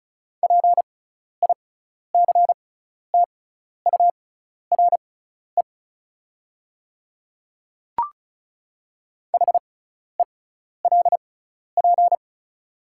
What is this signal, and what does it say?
Morse code, heard as a keyed tone, about 34 wpm, spaced as for 12 wpm, 700 Hz. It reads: PICTURE HELP
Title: 6x Character Spacing - Top 300 Words - 35wpm